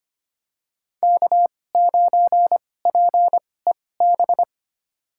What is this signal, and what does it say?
Morse code, heard as a keyed tone, about 25 wpm, spaced as for 18 wpm, 700 Hz. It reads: K9PEB
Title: USA Call Signs - 5 Characters - 2x Character Spacing - 25wpm